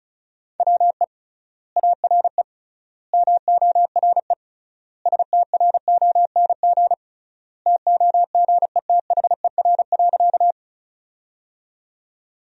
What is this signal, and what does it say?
Morse code, heard as a keyed tone, about 34 wpm, 700 Hz. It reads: WE ARE MORE STRONG TOGETHER.
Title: 3x Word-Spacing - Sentences from Top 500 Words - 35wpm